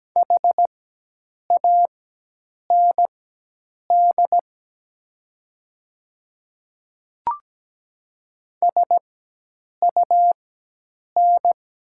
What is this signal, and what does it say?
Morse code, heard as a keyed tone, about 17 wpm, spaced as for 8 wpm, 700 Hz. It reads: HAND SUN
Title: 4x Character Spacing - Top 300 Words - 17wpm